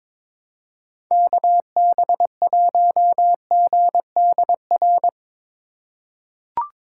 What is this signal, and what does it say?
Morse code, heard as a keyed tone, about 21 wpm, 700 Hz. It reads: KB1GDR
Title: USA Call Signs - 6 Characters - 3x Character Spacing - 22wpm